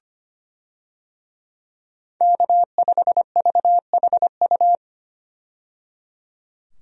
Morse code, heard as a keyed tone, about 25 wpm, 700 Hz. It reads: K5VHU
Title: USA Call Signs - 5 Characters - 25wpm